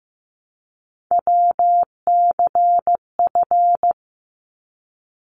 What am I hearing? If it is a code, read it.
Morse code, heard as a keyed tone, about 15 wpm, 700 Hz. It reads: WCF
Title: ARRL Sections 15wpm